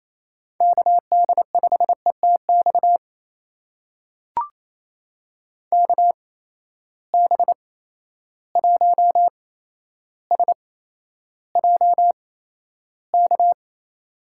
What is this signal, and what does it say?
Morse code, heard as a keyed tone, about 28 wpm, spaced as for 12 wpm, 700 Hz. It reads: KD5ETX KB1SJK
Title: USA Call Signs - 6 Characters - 8x Character Spacing - 28wpm